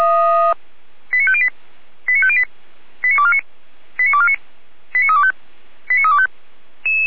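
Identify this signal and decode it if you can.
Five-tone selective call (ZVEI), heard as quick runs of five tones, tones of about 70 ms. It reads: 89508 89508 89249 89249 89325 89325
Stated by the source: ZVEI five-tone call 89250 received by SDR, fire brigade Orlamünde.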